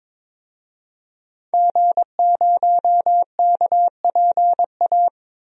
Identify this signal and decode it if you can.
Morse code, heard as a keyed tone, about 22 wpm, 700 Hz. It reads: G0KPA
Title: DXCC-100 Call Signs - 5 Characters - Repeated 2x - 22wpm